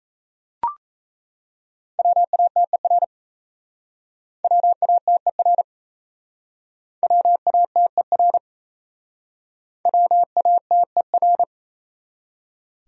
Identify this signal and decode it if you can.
Morse code, heard as a keyed tone, about 37 wpm, 700 Hz. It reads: WATER WATER WATER WATER